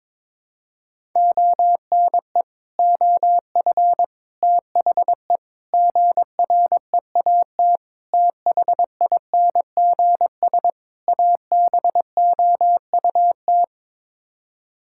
Morse code, heard as a keyed tone, about 22 wpm, 700 Hz. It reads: ONE OF THE GREAT THINGS ABOUT
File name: practice / 22wpm / Sets of 6 Words - 22wpm